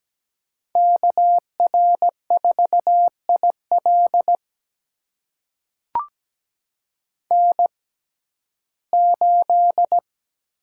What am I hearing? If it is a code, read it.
Morse code, heard as a keyed tone, about 17 wpm, spaced as for 12 wpm, 700 Hz. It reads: KR4IL N8
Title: USA Call Signs - 5 Characters - 6x Character Spacing - 17wpm